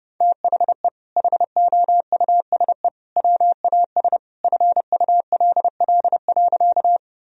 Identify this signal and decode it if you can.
Morse code, heard as a keyed tone, about 31 wpm, 700 Hz. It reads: THE HOUSE WAS FULL.
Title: Sentences from Top 1000 Words - Review Part 3 - 30wpm